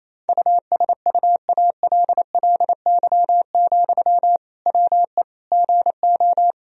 Morse code, heard as a keyed tone, about 28 wpm, 700 Hz. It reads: USUALLY, WE GO
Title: Sentences from Top 1000 Words - Review Part 3 - 28wpm